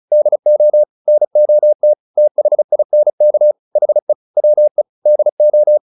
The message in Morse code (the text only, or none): DO NOT THINK HE WE DO